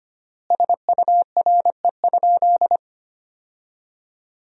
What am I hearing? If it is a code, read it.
Morse code, heard as a keyed tone, about 25 wpm, 700 Hz. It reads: SURE?
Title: 3x Word-Spacing - Sentences from Top 300 Words - 25wpm